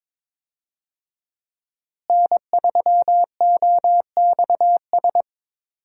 Morse code, heard as a keyed tone, about 22 wpm, 700 Hz. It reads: N3OXS
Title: USA Call Signs - 5 Characters - Repeated 2x - 22wpm